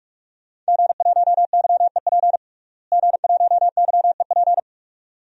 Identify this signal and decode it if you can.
Morse code, heard as a keyed tone, about 46 wpm, 700 Hz. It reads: G1YEP G1YEP
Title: DXCC-100 Call Signs - 5 Characters - Repeated 2x - 45wpm